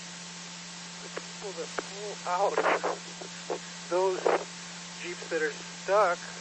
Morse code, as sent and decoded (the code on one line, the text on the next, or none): none